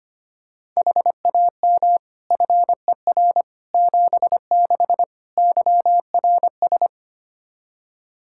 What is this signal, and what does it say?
Morse code, heard as a keyed tone, about 25 wpm, 700 Hz. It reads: HAM FER 76 YRS
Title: Practice QSO - Exchange 3 - More Info - 25wpm